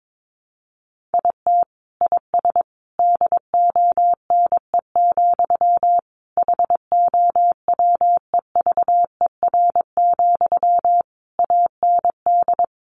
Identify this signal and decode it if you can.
Morse code, heard as a keyed tone, about 22 wpm, 700 Hz. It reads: IT IS DONE, HOWEVER, AND
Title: Sentences from Top 600 Words 22wpm